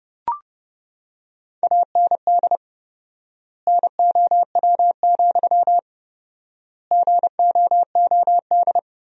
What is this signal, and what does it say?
Morse code, heard as a keyed tone, about 30 wpm, 700 Hz. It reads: AND NOW, GOOD
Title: 4x Word-Spacing - Sentences from Top 300 Words - 30wpm